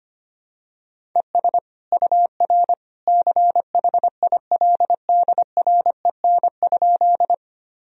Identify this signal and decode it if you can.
Morse code, heard as a keyed tone, about 25 wpm, 700 Hz. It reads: ES UR CHILDREN?
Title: Sentences from Top 300 Words and Abbreviations 25wpm